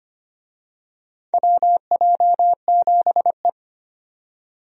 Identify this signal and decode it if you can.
Morse code, heard as a keyed tone, about 25 wpm, 700 Hz. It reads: WJ7E
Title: USA Call Signs - 4 Characters - 25wpm